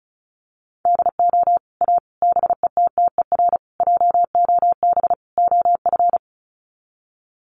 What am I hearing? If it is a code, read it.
Morse code, heard as a keyed tone, about 34 wpm, 700 Hz. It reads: DO A BETTER JOB OF
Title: Sets of 5 Words Encore2 35wpm